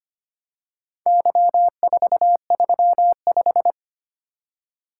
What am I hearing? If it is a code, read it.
Morse code, heard as a keyed tone, about 25 wpm, 690 Hz. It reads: Y435
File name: practice / 25wpm / Four Letter-Number Combinations - 25wpm